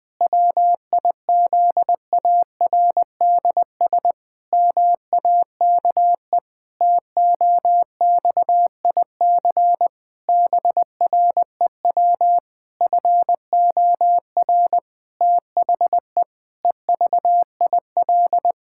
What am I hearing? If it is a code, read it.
Morse code, heard as a keyed tone, about 21 wpm, 700 Hz. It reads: WIZARDS MAKE TOXIC BREW FOR THE EVIL
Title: Pangram Challenge v2 - 20wpm